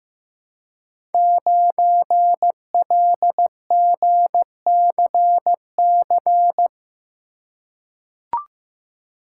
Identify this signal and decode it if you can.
Morse code, heard as a keyed tone, about 15 wpm, 700 Hz. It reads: 9LGCC
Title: Code Groups - 5 Letter:Numbers - 15wpm